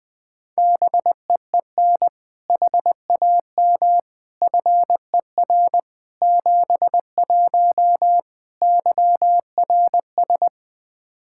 Morse code, heard as a keyed tone, about 20 wpm, 700 Hz. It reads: BEEN HAM FER 71 YRS
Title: Practice QSO - Exchange 3 - More Info - 20wpm